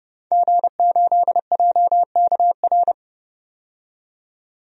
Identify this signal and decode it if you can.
Morse code, heard as a keyed tone, about 30 wpm, 700 Hz. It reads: G8JKR
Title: DXCC-100 Call Signs - 5 Characters - 30wpm